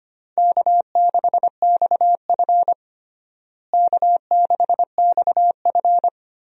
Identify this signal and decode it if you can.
Morse code, heard as a keyed tone, about 25 wpm, 700 Hz. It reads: K6XF K6XF
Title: USA Call Signs - 4 Characters - Repeated 2x - 25wpm